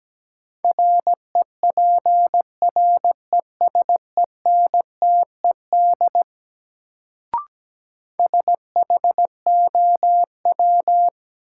Morse code, heard as a keyed tone, about 17 wpm, 700 Hz. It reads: REPRESENTED SHOW
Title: Sets of 5 Words - New Word Drill - 17wpm